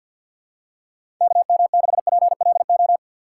Morse code, heard as a keyed tone, about 48 wpm, 700 Hz. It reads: KM6PLO
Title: USA Call Signs - 6 Characters - 3x Character Spacing - 50wpm